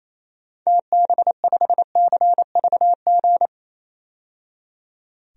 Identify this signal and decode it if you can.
Morse code, heard as a keyed tone, about 28 wpm, 700 Hz. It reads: TB5CVG
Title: DXCC-100 Call Signs - 6 Characters - 28wpm